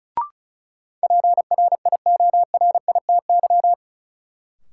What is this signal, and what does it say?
Morse code, heard as a keyed tone, about 36 wpm, 700 Hz. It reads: PRIORITY
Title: Sentences from Top 1600 Words - New Word Drill Alt - 35wpm